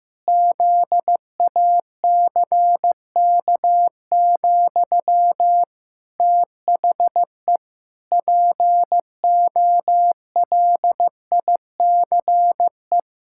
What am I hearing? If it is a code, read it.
Morse code, heard as a keyed tone, about 15 wpm, 700 Hz. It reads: ZACK, THE POLICE
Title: Pangram Challenge v2 - 15wpm